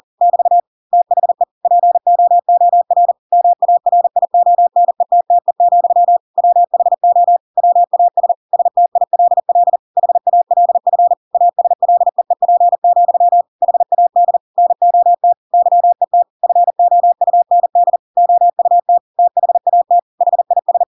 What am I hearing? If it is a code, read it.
Morse code, heard as a keyed tone, about 40 wpm, 700 Hz. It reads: <BT> THE POOR MARIONETTE, WHO WAS STILL HALF ASLEEP, HAD NOT YET FOUND OUT THAT HIS
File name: Pinocchio / Pinocchio ch7 40 40 ik1hge